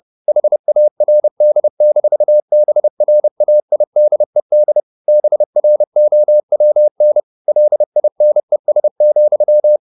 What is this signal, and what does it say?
Morse code, heard as a keyed tone, about 30 wpm, 600 Hz. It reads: HARD-BRAIDED BROWN LINES,
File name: the old man and the sea / the old man and the sea 1 30wpm